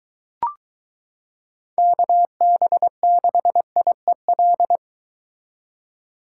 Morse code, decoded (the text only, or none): KB6IEL